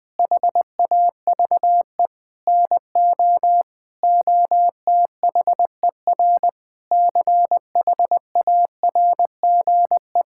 Morse code, heard as a keyed tone, about 21 wpm, 700 Hz. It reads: HAVE NO OTHER CHARGE
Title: Sentences from Top 1000 Words - Review Part 6 - 20wpm